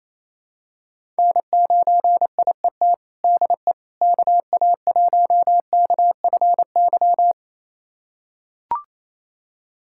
Morse code, heard as a keyed tone, about 28 wpm, 700 Hz. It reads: N9IET DE KA1KFY
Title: USA Callsign DE Callsign 28wpm